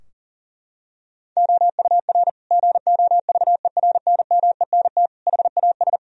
Morse code, heard as a keyed tone, about 40 wpm, 700 Hz. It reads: OUR GOVERNMENT HAS
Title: Sentences from Top 1000 Words - Review Part 3 - 40wpm